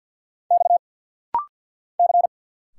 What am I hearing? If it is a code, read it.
Morse code, heard as a keyed tone, about 50 wpm, 700 Hz. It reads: X C
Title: Single Letters - Rapid-Fire 50wpm